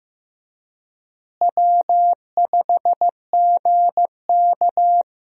Morse code, heard as a keyed tone, about 15 wpm, 700 Hz. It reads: W5GK
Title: USA Call Signs - 4 Characters - Repeated 3x - 15wpm